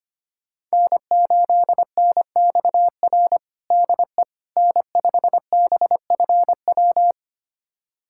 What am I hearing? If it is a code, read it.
Morse code, heard as a keyed tone, about 25 wpm, 700 Hz. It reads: N8NXR DE N5BFW
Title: USA Callsign DE Callsign - 5 Characters - 25wpm